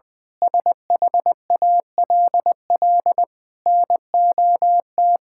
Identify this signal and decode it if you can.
Morse code, heard as a keyed tone, about 20 wpm, 700 Hz. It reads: SHALL NOT
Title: Sentences with 9-Word Phrases - Spoken Beginning - 20wpm